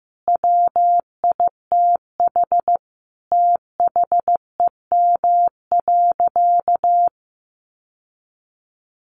Morse code, heard as a keyed tone, about 15 wpm, 700 Hz. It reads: WITH THEM.
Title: Sentences from Top 800 Words 15wpm